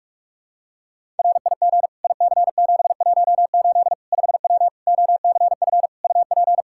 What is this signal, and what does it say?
Morse code, heard as a keyed tone, about 45 wpm, 700 Hz. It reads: RIG IC718 5W OCF UP
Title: K1USN SST QSO Practice 45wpm